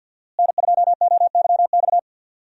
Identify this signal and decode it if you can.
Morse code, heard as a keyed tone, about 52 wpm, 700 Hz. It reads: N2OYX
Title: USA Call Signs - 5 Characters - Repeated 2x - 50wpm